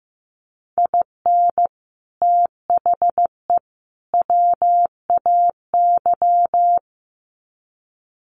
Morse code, heard as a keyed tone, about 15 wpm, 700 Hz. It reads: IN THE WAY
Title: Sets of 3 Words 15wpm